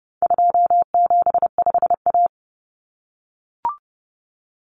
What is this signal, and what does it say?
Morse code, heard as a keyed tone, about 30 wpm, 700 Hz. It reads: 275A
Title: Common Cut Numbers - 4-Digit 30wpm